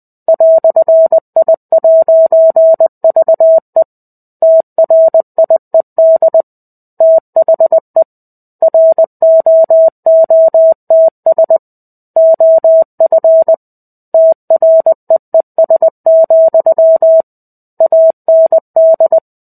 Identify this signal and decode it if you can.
Morse code, heard as a keyed tone, about 20 wpm, 650 Hz. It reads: "I'VE TRIED THE ROOTS OF TREES, AND
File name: 05-Chapter-05 20 WPM